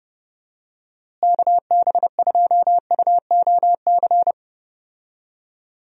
Morse code, heard as a keyed tone, about 32 wpm, 700 Hz. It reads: KB2UOC
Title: USA Call Signs - 6 Characters - Repeated 2x - 30wpm